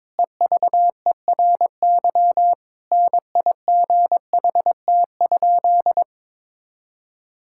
Morse code, heard as a keyed tone, about 22 wpm, 700 Hz. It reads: EVERY NIGHT?
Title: Sentences from Top 1000 Words - Review Part 5 - 22wpm